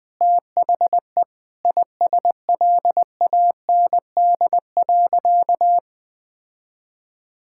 Morse code, heard as a keyed tone, about 20 wpm, 700 Hz. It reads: THE ISLAND.
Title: Sentences from Top 1000 Words - Review Part 6 - 20wpm